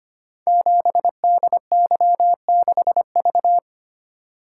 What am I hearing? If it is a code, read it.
Morse code, heard as a keyed tone, about 25 wpm, 700 Hz. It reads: 7DY6V
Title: Code Groups - 5 Letter:Numbers - 25wpm